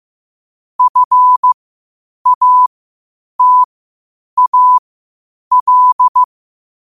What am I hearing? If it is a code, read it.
Morse code, heard as a keyed tone, about 15 wpm, spaced as for 8 wpm, 1000 Hz. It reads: FATAL
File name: qotdfiles / 8wpm